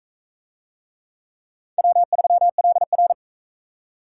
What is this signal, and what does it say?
Morse code, heard as a keyed tone, about 40 wpm, 700 Hz. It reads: W3LR